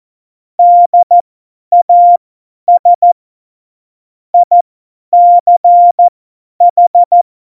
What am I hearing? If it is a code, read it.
Morse code, heard as a keyed tone, about 14 wpm, spaced as for 10 wpm, 700 Hz. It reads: DAS ICH